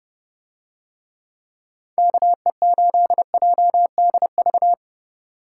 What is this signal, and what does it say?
Morse code, heard as a keyed tone, about 30 wpm, 700 Hz. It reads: KE8JDV